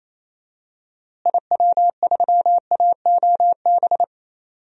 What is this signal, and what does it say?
Morse code, heard as a keyed tone, about 28 wpm, 700 Hz. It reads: IW3AOB
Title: DXCC-100 Call Signs - 6 Characters - Repeated 2x - 28wpm